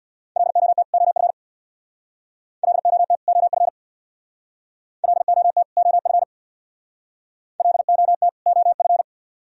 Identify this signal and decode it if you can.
Morse code, heard as a keyed tone, about 63 wpm, 700 Hz. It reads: LOT OF LOT OF LOT OF LOT OF